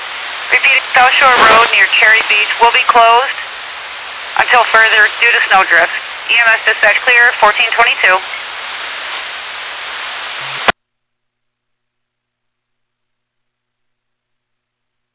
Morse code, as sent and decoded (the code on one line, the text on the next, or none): none